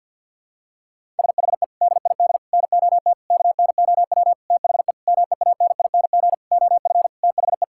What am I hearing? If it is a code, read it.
Morse code, heard as a keyed tone, about 48 wpm, 700 Hz. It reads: SHE DID NOT KNOW THE MEANING OF THE
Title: Sentences from Top 1000 Words - Review Part 6 - 50wpm